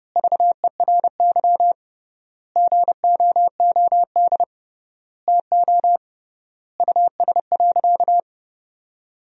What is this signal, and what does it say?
Morse code, heard as a keyed tone, about 32 wpm, 700 Hz. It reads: VERY GOOD TO US.
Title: 3x Word-Spacing - Sentences from Top 200 Words - 30wpm